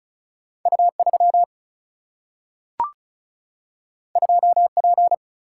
Morse code, heard as a keyed tone, about 36 wpm, 700 Hz. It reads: U3 2P
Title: Single-Letter Single-Number 35wpm